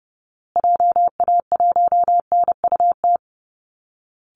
Morse code, heard as a keyed tone, about 30 wpm, 700 Hz. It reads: JA1NUT
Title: Call Signs Repeated 2x - 30wpm